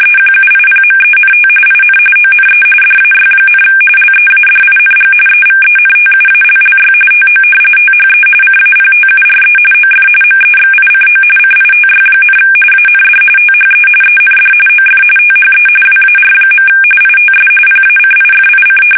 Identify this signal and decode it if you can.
RTTY, about 45.45 baud. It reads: EQUIPMENT IN ONE DAY, LEADING EVENTUALLY TO THE CAPTURE OF MEXICO CITY, AND THE END OF THE WAR. ITS PACIFIC
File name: fsk850 / usnavy2